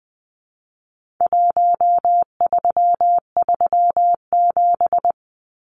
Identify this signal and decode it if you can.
Morse code, heard as a keyed tone, about 20 wpm, 700 Hz. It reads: 1337
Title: Numbers - 4-Digit 20wpm